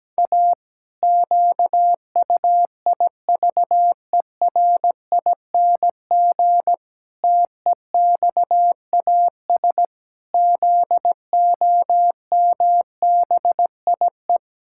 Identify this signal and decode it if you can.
Morse code, heard as a keyed tone, about 17 wpm, 700 Hz. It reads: A QUIVERING TEXAS ZOMBIE
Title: Pangram Challenge v2 - 17wpm